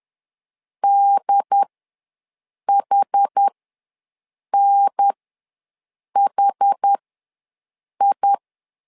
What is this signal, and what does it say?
Morse code, heard as a keyed tone, about 10 wpm, spaced as for 6 wpm, 800 Hz. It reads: DHNHI
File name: 2 UHDN ws 10wpm